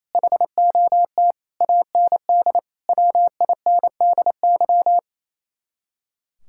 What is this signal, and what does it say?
Morse code, heard as a keyed tone, about 33 wpm, 700 Hz. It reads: HOT AND WINDY
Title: Speed-Racing Common QSO Phrases - 28wpm